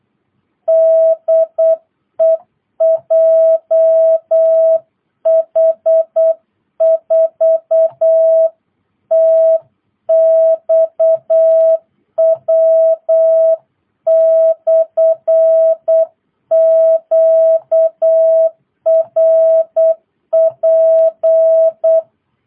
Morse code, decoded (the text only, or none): DE JH4TXW/QRP